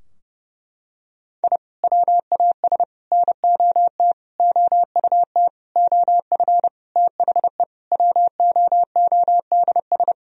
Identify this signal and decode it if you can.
Morse code, heard as a keyed tone, about 32 wpm, 700 Hz. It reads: I WAS NOT OUT OF THE WOODS
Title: Sentences from Top 1000 Words - Review Part 2 - 30wpm